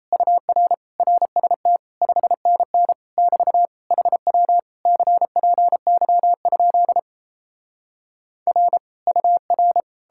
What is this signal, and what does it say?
Morse code, heard as a keyed tone, about 37 wpm, 700 Hz. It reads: UR RST 5NN <BT> HW CPY? R UR